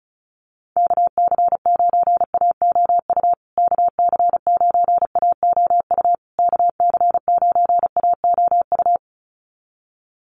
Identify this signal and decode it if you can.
Morse code, heard as a keyed tone, about 34 wpm, 700 Hz. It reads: KC9AOU KC9AOU KC9AOU
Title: USA Call Signs Repeated 3x 35wpm